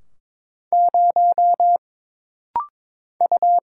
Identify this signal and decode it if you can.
Morse code, heard as a keyed tone, about 22 wpm, 700 Hz. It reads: 0 U
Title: Single Letter-Number - Rapid-Fire 22wpm